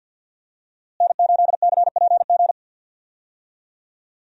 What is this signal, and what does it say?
Morse code, heard as a keyed tone, about 52 wpm, 700 Hz. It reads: N8CPG
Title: USA Call Signs - 5 Characters - Repeated 3x - 50wpm